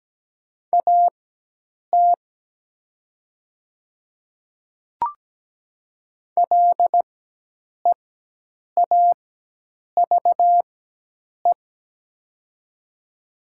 Morse code, heard as a keyed tone, about 17 wpm, spaced as for 8 wpm, 700 Hz. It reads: AT LEAVE
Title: 4x Character Spacing - Top 500 Words - 17wpm